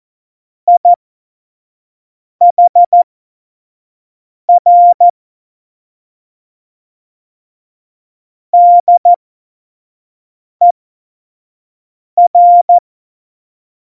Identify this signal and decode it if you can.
Morse code, heard as a keyed tone, about 14 wpm, spaced as for 5 wpm, 700 Hz. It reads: IHR DER
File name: Morsekurs / Witze / Witz 21 Juli 2020, 25 F70